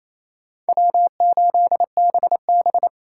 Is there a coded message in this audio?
Yes: Morse code from 0.7 to 2.9 s, heard as a keyed tone, about 37 wpm, 700 Hz. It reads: W8BB